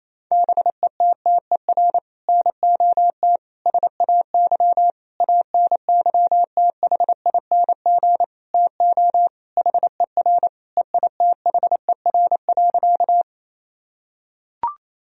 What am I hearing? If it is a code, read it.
Morse code, heard as a keyed tone, about 27 wpm, 700 Hz. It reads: BETTER NOT SAY ANYTHING TO HER EITHER.
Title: Sentences from Top 1000 Words - Review Part 4 - 28wpm